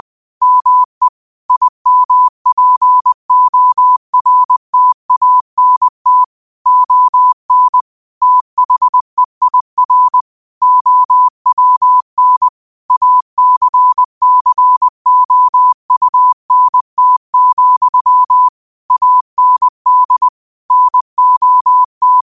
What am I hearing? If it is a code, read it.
Morse code, heard as a keyed tone, about 20 wpm, 1000 Hz. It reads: ME IMPORTANT ON THEIR OWN ACCOUNT, AND NOT
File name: qotdfiles / 20wpm